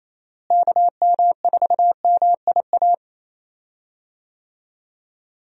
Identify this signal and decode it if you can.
Morse code, heard as a keyed tone, about 28 wpm, 700 Hz. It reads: KM4MIA
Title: USA Call Signs - 6 Characters - Repeated 3x - 28wpm